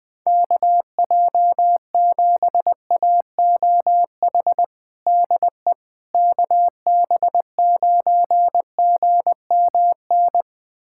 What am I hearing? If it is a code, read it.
Morse code, heard as a keyed tone, about 19 wpm, 700 Hz. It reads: KJ7AOH DE KB9GMN
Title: USA Callsign DE Callsign - 6 Characters - 20wpm